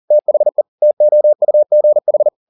Morse code, heard as a keyed tone, about 40 wpm, 600 Hz. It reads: THE TOUGH